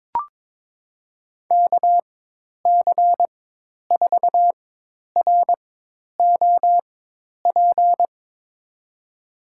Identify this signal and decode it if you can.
Morse code, heard as a keyed tone, about 22 wpm, spaced as for 10 wpm, 700 Hz. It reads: KC4ROP
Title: USA Call Signs - 4x Character Spacing - 22wpm